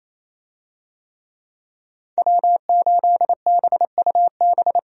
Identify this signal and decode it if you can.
Morse code, heard as a keyed tone, about 27 wpm, 700 Hz. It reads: W8BUB
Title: USA Call Signs - 5 Characters - Repeated 2x - 28wpm